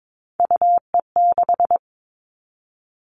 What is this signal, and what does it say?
Morse code, heard as a keyed tone, about 22 wpm, 700 Hz. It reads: UE6
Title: All Cut Numbers - 3-Digit 22wpm